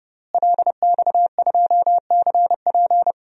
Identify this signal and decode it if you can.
Morse code, heard as a keyed tone, about 30 wpm, 700 Hz. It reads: LX2CP
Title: DXCC-100 Call Signs - 5 Characters - Repeated 3x - 30wpm